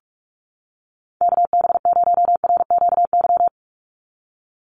Morse code, heard as a keyed tone, about 45 wpm, 700 Hz. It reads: KB0RQY
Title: USA Call Signs 45wpm